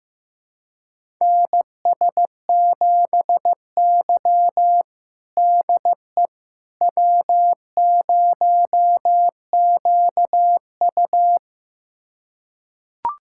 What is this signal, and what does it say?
Morse code, heard as a keyed tone, about 15 wpm, 700 Hz. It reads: NS7Y DE W0QU
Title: USA Callsign DE Callsign - 4 Characters - 15wpm